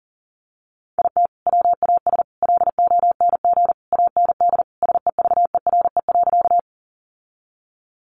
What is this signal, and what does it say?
Morse code, heard as a keyed tone, about 40 wpm, 700 Hz. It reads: IT WAS LONG AND SEVERE.
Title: Sentences from Top 1400 Words 40wpm